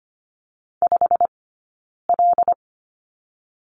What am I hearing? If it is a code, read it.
Morse code, heard as a keyed tone, about 25 wpm, spaced as for 9 wpm, 700 Hz. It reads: 5L
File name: Lesson 15d - Number 5 - Two Characters 25wpm